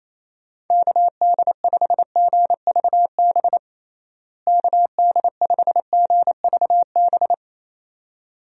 Morse code, heard as a keyed tone, about 28 wpm, 700 Hz. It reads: KD5GVB KD5GVB
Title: USA Call Signs - 6 Characters - Repeated 2x - 28wpm